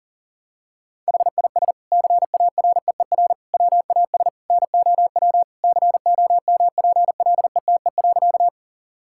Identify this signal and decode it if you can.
Morse code, heard as a keyed tone, about 40 wpm, 700 Hz. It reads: HIS CAREER WAS NOW COMPLETE.
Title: Sentences from Top 1000 Words - Review Part 6 - 40wpm